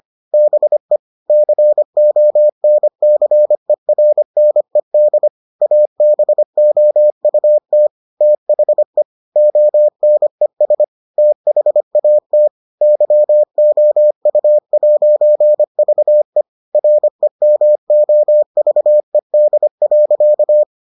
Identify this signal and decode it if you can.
Morse code, heard as a keyed tone, about 25 wpm, 600 Hz. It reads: BE CONCERNED ABOUT THE ONES THAT YOU'VE REMOVED.